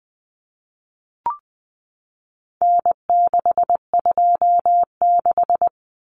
Morse code, heard as a keyed tone, about 20 wpm, 700 Hz. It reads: N626
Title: Common Cut Numbers - 4-Digit 20wpm